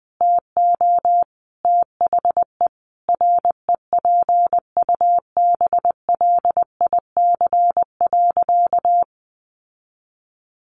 Morse code, heard as a keyed tone, about 20 wpm, 700 Hz. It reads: TO THE REPUBLIC.
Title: Sentences from Top 1500 Words 20wpm